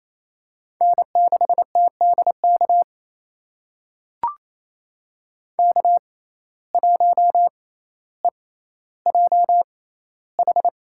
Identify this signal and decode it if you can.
Morse code, heard as a keyed tone, about 28 wpm, spaced as for 14 wpm, 700 Hz. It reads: N6TDK K1EJH
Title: USA Call Signs - 5 Characters - 6x Character Spacing - 28wpm